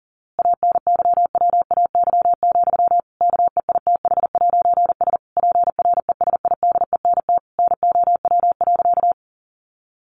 Morse code, heard as a keyed tone, about 40 wpm, 700 Hz. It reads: ANYWAY, KEITH'S PRESIDENT NOW.
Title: A Sound Of Thunder 40wpm